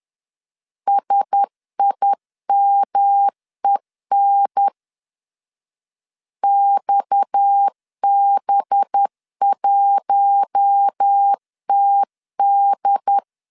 Morse code, heard as a keyed tone, about 10 wpm, 800 Hz. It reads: SIMEN XB1TD